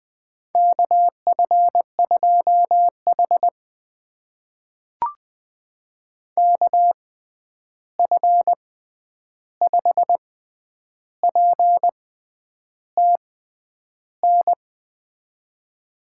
Morse code, heard as a keyed tone, about 21 wpm, spaced as for 9 wpm, 700 Hz. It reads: KF2H KF5PTN
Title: USA Call Signs - 6x Character Spacing - 20wpm